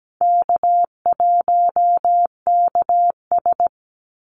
Morse code, heard as a keyed tone, about 17 wpm, 700 Hz. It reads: K1KS
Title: Easy Call Signs - 17wpm